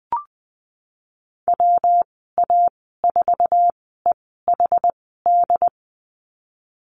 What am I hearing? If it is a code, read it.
Morse code, heard as a keyed tone, about 20 wpm, spaced as for 15 wpm, 700 Hz. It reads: WA4EHD
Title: 179 Lesson 29g - Slash - Call Signs 20wpm